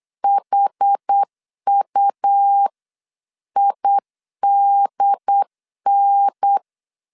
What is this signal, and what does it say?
Morse code, heard as a keyed tone, about 8 wpm, 800 Hz. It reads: HUIDN